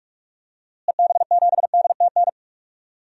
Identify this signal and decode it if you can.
Morse code, heard as a keyed tone, about 45 wpm, 700 Hz. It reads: EB7DTN
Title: DXCC-100 Call Signs - 6 Characters - Repeated 2x - 45wpm